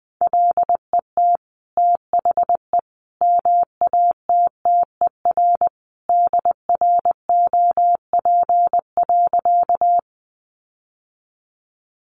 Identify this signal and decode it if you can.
Morse code, heard as a keyed tone, about 20 wpm, 700 Hz. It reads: LET THE MATTER DROP.
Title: Sentences from Top 1400 Words 20wpm